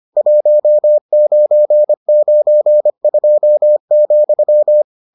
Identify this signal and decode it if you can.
Morse code, heard as a keyed tone, about 25 wpm, 600 Hz. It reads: 1992,